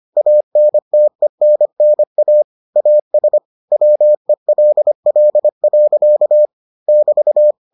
Morse code, heard as a keyed tone, about 25 wpm, 600 Hz. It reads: ANTENNA AS WELL. <BT>